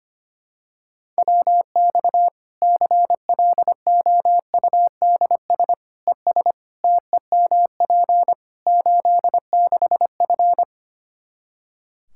Morse code, heard as a keyed tone, about 25 wpm, 700 Hz. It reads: WX CLOUDS ES TEMP 86F
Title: Practice QSO - Exchange 2 - Info - 25wpm